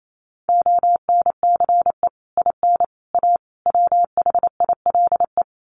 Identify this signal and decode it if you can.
Morse code, heard as a keyed tone, about 28 wpm, 700 Hz. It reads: ONCE IN A WHILE
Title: Sets of 4 Words 28wpm